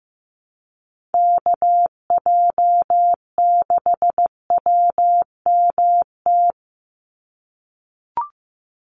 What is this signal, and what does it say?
Morse code, heard as a keyed tone, about 15 wpm, 700 Hz. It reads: KJ6WMT
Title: USA Call Signs 15wpm